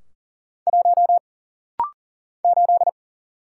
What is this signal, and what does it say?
Morse code, heard as a keyed tone, about 40 wpm, 700 Hz. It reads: 1 8
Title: Single Letter-Number - Rapid-Fire 40wpm